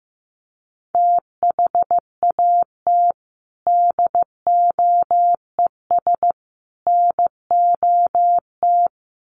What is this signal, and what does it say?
Morse code, heard as a keyed tone, about 15 wpm, 700 Hz. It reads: THAT DOES NOT
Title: Sets of 3 Words 15wpm